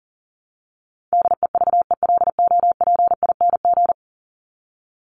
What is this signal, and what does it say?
Morse code, heard as a keyed tone, about 40 wpm, 700 Hz. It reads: DEVELOPING
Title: ING Suffix - Top 100 Words with Base Words 40wpm